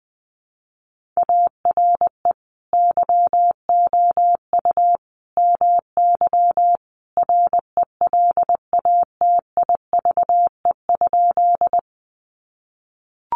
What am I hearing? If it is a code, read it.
Morse code, heard as a keyed tone, about 20 wpm, 700 Hz. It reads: ARE YOU MY RELATIVE?
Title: Sentences from Top 1400 Words 20wpm